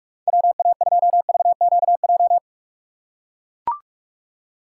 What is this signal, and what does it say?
Morse code, heard as a keyed tone, about 41 wpm, 700 Hz. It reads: WA2VQJ
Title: USA Call Signs - 6 Characters - 2x Character Spacing - 45wpm